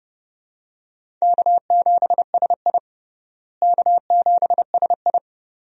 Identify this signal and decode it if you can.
Morse code, heard as a keyed tone, about 31 wpm, 700 Hz. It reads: K7SI K7SI
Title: USA Call Signs - 4 Characters - Repeated 2x - 30wpm